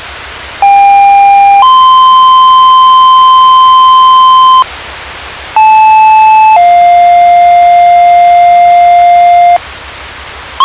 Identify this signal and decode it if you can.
Two-tone page, heard as a one-second tone, then a three-second tone, about 800 then 1040 Hz, about 870 then 700 Hz. It is heard from 0.6 to 9.6 s.